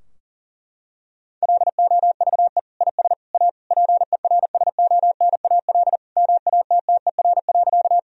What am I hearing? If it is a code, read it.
Morse code, heard as a keyed tone, about 40 wpm, 700 Hz. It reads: LOVE IS A PERSONAL MATTER.
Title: Sentences from Top 1000 Words - Review Part 4 - 40wpm